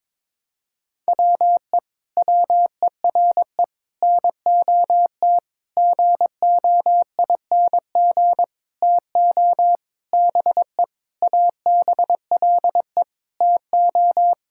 Morse code, heard as a keyed tone, about 22 wpm, 700 Hz. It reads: WE WERE NOT GOING TO BE ABLE TO
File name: Sentences with 7-Word Phrases - 22wpm